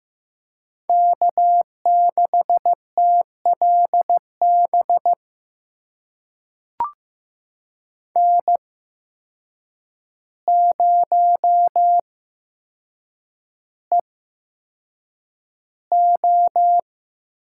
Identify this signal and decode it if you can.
Morse code, heard as a keyed tone, about 15 wpm, spaced as for 7 wpm, 700 Hz. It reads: K6TLB N0EO
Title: USA Call Signs - 5 Characters - 8x Character Spacing - 15wpm